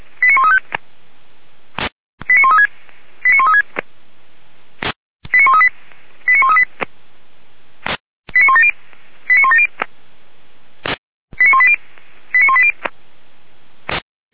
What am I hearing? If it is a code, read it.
Five-tone selective call (ZVEI), heard as quick runs of five tones, tones of about 70 ms. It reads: 89135 89136 89136 89138 89138 89170 89170 89180 89180